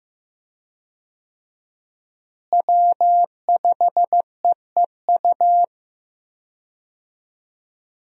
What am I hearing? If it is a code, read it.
Morse code, heard as a keyed tone, about 15 wpm, 700 Hz. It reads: W5EEU
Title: USA Call Signs - 5 Characters - Repeated 2x - 15wpm